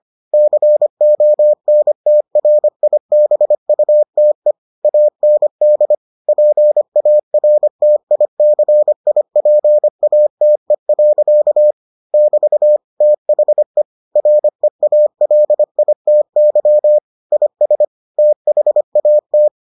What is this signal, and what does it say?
Morse code, heard as a keyed tone, about 25 wpm, 600 Hz. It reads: CONTRIBUTE AND PARTICIPATE. <BT> THE REALITY IS THAT